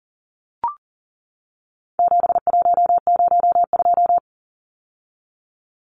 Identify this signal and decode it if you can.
Morse code, heard as a keyed tone, about 39 wpm, 700 Hz. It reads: 7102
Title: Numbers - 4-Digit 40wpm